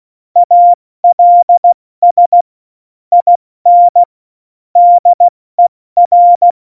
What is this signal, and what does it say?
Morse code, heard as a keyed tone, about 16 wpm, 700 Hz. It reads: ALS IN DER
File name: Morsekurs / Witze / Witz 07 Juli 2020, 70 F80